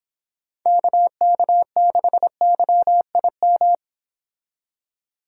Morse code, heard as a keyed tone, about 26 wpm, 700 Hz. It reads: KK6YIM